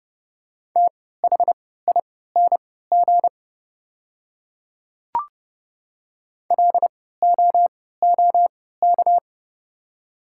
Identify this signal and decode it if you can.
Morse code, heard as a keyed tone, about 30 wpm, spaced as for 17 wpm, 700 Hz. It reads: THING LOOK
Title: 3x Character Spacing - Top 100 Words - 30wpm